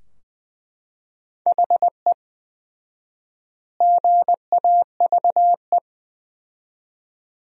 Morse code, heard as a keyed tone, about 20 wpm, 700 Hz. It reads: HE GAVE
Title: 4x Word-Spacing - Sentences from Top 500 Words - 20wpm